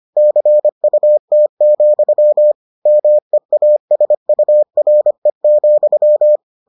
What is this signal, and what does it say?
Morse code, heard as a keyed tone, about 25 wpm, 600 Hz. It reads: CUT, MEASURE,